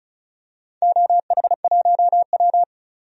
Morse code, heard as a keyed tone, about 34 wpm, 700 Hz. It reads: OH1W
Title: DXCC-100 Call Signs - 4 Characters - 35wpm